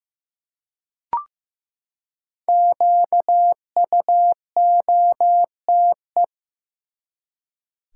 Morse code, heard as a keyed tone, about 15 wpm, 700 Hz. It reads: QUOTE
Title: Five Letter Words - 15wpm